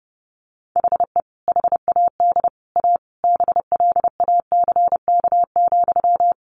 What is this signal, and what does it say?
Morse code, heard as a keyed tone, about 30 wpm, 700 Hz. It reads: HE HAD A BLACK,
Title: Sentences from Top 1500 Words 30wpm